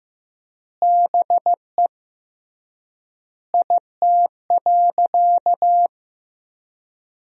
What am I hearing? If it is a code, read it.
Morse code, heard as a keyed tone, about 16 wpm, 700 Hz. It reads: BE IT.
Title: 3x Word-Spacing - Sentences from Top 100 Words - 15wpm